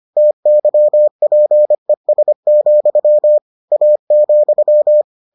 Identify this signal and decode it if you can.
Morse code, heard as a keyed tone, about 25 wpm, 600 Hz. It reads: TYPES, A,